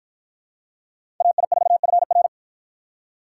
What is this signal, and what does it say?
Morse code, heard as a keyed tone, about 48 wpm, 700 Hz. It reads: AI4FR